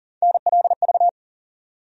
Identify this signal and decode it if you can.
Morse code, heard as a keyed tone, about 40 wpm, 700 Hz. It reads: NLV
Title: USA Call Sign Postfixes - 3 Letters 40wpm